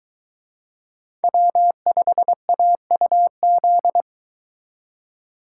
Morse code, heard as a keyed tone, about 23 wpm, 700 Hz. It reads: W5AUZ